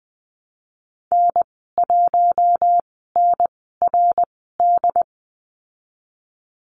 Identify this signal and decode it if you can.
Morse code, heard as a keyed tone, about 19 wpm, spaced as for 14 wpm, 700 Hz. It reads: N1NRD
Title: 200 Lesson 32g - Letter k - Call Signs 20wpm